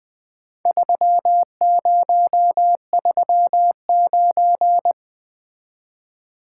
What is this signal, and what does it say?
Morse code, heard as a keyed tone, about 20 wpm, 700 Hz. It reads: 3039
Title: Four Letter-Number Combinations - 20wpm